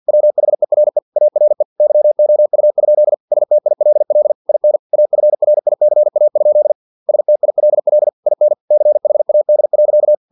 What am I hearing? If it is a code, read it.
Morse code, heard as a keyed tone, about 50 wpm, 600 Hz. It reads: WHERE ARE YOU? STILL IN AFRICA? STILL IN CHAD.
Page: Never 49wpm